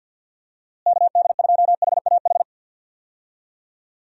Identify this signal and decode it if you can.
Morse code, heard as a keyed tone, about 50 wpm, 700 Hz. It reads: KD2HAH
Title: USA Call Signs - 6 Characters - Repeated 3x - 50wpm